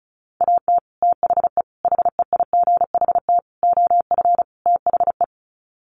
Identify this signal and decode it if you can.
Morse code, heard as a keyed tone, about 34 wpm, 700 Hz. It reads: AT THE HEIGHT OF THE
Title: Sets of 5 Words 35wpm